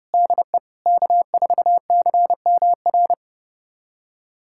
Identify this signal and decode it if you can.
Morse code, heard as a keyed tone, about 32 wpm, 700 Hz. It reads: DE K4CMR
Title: USA Callsign DE Callsign - 5 Characters - 30wpm